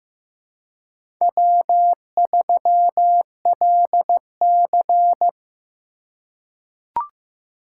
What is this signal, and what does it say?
Morse code, heard as a keyed tone, about 15 wpm, 700 Hz. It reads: W3LC
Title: USA Call Signs - 4 Characters - 4x Character Spacing - 15wpm